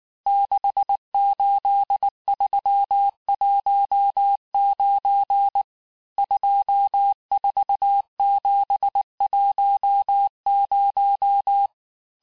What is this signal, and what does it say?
Morse code, heard as a keyed tone, about 19 wpm, 800 Hz. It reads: 68319 24710
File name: cours-du-lundi files / cours-1595-teste-CHIFFRE-vitesse18-MPM